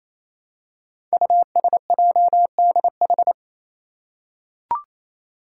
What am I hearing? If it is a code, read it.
Morse code, heard as a keyed tone, about 27 wpm, 700 Hz. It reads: USJDH
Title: Code Groups - 5 Letters - 28wpm